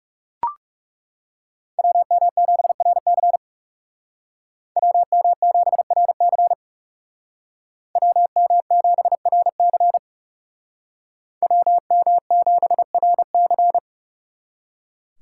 Morse code, heard as a keyed tone, about 37 wpm, 700 Hz. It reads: WM7RC WM7RC WM7RC WM7RC
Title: Speed Racing - USA 2x2 Call Signs 30wpm